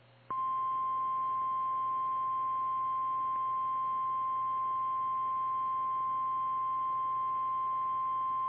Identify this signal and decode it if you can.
Emergency Alert System (SAME) tones (the 1050 Hz attention tone). It is heard from 1.0 to 8.1 s.